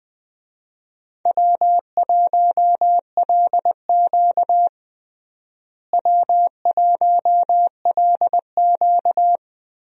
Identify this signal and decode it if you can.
Morse code, heard as a keyed tone, about 21 wpm, 700 Hz. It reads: W1LQ W1LQ
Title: USA Call Signs - 4 Characters - Repeated 3x - 20wpm